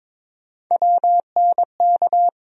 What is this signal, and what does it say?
Morse code, heard as a keyed tone, about 21 wpm, 700 Hz. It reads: WNK
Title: USA Call Sign Postfixes - 3 Letters 22wpm